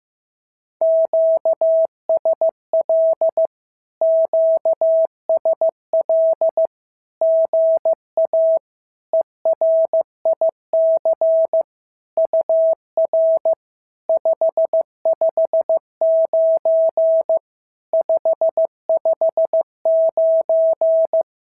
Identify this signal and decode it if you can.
Morse code, heard as a keyed tone, about 15 wpm, 650 Hz. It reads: QSL QSL GA ERIC UR 559 559
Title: SOTA QSO Practice 15wpm